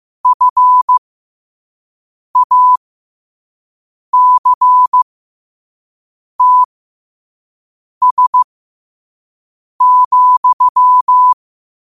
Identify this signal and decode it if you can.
Morse code, heard as a keyed tone, about 15 wpm, spaced as for 5 wpm, 1000 Hz. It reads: FACTS,
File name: qotdfiles / 5wpm